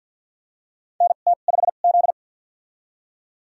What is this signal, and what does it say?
Morse code, heard as a keyed tone, about 52 wpm, spaced as for 37 wpm, 700 Hz. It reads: NT5Z
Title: USA Call Signs - 4 Characters - 2x Character Spacing - 50wpm